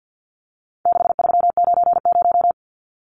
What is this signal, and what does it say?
Morse code, heard as a keyed tone, about 50 wpm, 700 Hz. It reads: 6390
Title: Numbers - 4-Digit 50wpm